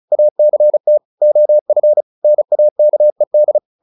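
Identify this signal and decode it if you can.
Morse code, heard as a keyed tone, about 34 wpm, 600 Hz. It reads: ACT OF NAKED
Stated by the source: Morse practice at 35 wpm from CW med Gnister, sending Ken Follett's novel Never in English.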